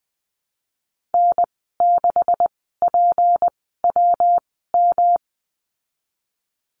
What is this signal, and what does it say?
Morse code, heard as a keyed tone, about 20 wpm, spaced as for 15 wpm, 700 Hz. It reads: N6PWM
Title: 167 Lesson 27g - Number 7 - Call Signs 20wpm